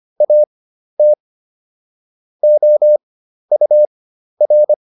Morse code, heard as a keyed tone, about 25 wpm, spaced as for 12 wpm, 600 Hz. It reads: AT OUR